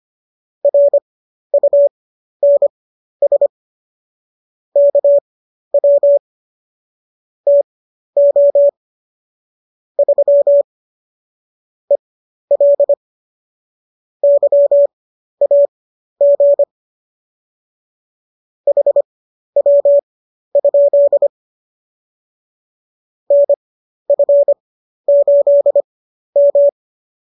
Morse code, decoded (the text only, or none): RUNS KW TO 3 EL YAG HW? NF8M